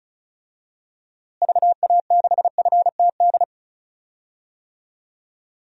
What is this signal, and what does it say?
Morse code, heard as a keyed tone, about 36 wpm, 700 Hz. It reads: VA6FTD
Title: DXCC-100 Call Signs - 6 Characters - Repeated 3x - 35wpm